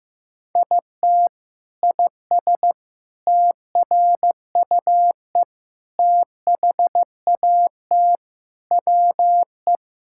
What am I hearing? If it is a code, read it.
Morse code, heard as a keyed tone, about 15 wpm, 700 Hz. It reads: IT IS TRUE THAT WE
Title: Sentences from Top 1600 Words 15wpm